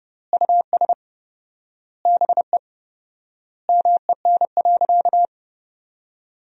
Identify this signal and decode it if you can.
Morse code, heard as a keyed tone, about 30 wpm, 700 Hz. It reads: US BE MEN.
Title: 4x Word-Spacing - Sentences from Top 300 Words - 30wpm